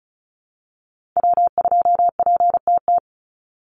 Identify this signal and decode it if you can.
Morse code, heard as a keyed tone, about 34 wpm, 700 Hz. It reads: W2PTT
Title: Call Signs - 35wpm